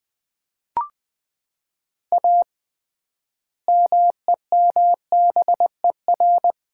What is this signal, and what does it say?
Morse code, heard as a keyed tone, about 20 wpm, 700 Hz. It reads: A MEMBER
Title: 3x Word-Spacing - Sets of 3 Words - 20wpm